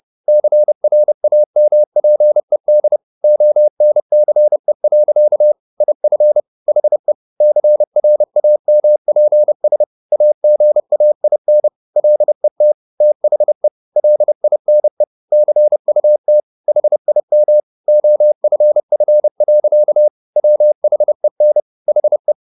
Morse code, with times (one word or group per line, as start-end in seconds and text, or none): 0.3-3.0 CRAMPED
3.2-5.5 ONCE.
5.8-6.4 IF
6.7-7.1 HE
7.4-9.8 CRAMPS
10.1-11.7 AGAIN
12.0-12.7 LET
13.0-13.7 THE
14.0-15.0 LINE
15.3-16.4 CUT
16.7-17.6 HIM
17.9-20.1 OFF.
20.4-21.6 WHEN
21.9-22.3 HE